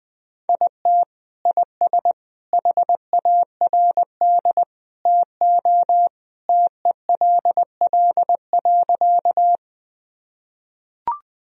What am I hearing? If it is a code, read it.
Morse code, heard as a keyed tone, about 21 wpm, 700 Hz. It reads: IT IS HARD TO TELL.
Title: Sentences from Top 1000 Words - Review Part 1 - 20wpm